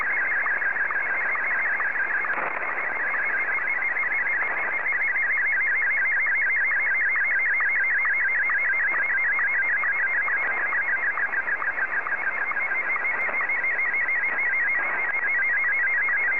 SSTV (slow-scan television), Martin 1.
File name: M1-VK3